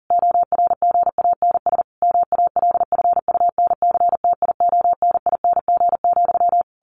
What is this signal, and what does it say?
Morse code, heard as a keyed tone, about 40 wpm, 700 Hz. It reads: ORGANS MALFUNCTIONING,
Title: A Sound Of Thunder 40wpm